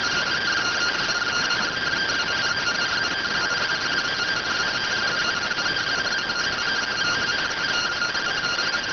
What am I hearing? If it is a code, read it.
Two-tone teleprinter signal, about 170 Hz shift.